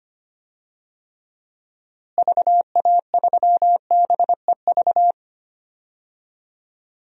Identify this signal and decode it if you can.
Morse code, heard as a keyed tone, about 25 wpm, 700 Hz. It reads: VA3BEV
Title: DXCC-100 Call Signs - 6 Characters - Repeated 3x - 25wpm